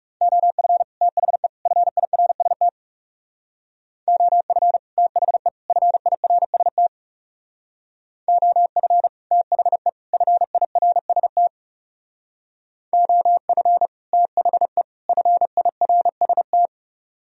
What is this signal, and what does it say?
Morse code, heard as a keyed tone, about 40 wpm, 700 Hz. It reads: OF THE FIRST OF THE FIRST OF THE FIRST OF THE FIRST